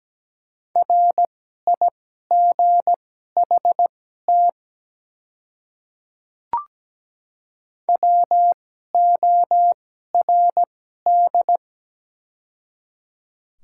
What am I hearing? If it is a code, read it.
Morse code, heard as a keyed tone, about 17 wpm, spaced as for 12 wpm, 700 Hz. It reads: RIGHT WORD
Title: 2x Character Spacing - Top 200 Words - 17wpm